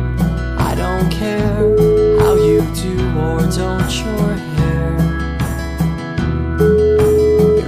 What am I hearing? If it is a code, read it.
Ringback tone (Europe, 425 Hz), with noise.